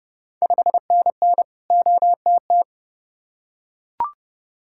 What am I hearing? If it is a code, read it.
Morse code, heard as a keyed tone, about 30 wpm, 700 Hz. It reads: 5NN OTT
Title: Ontario QSO Party Exchange 30wpm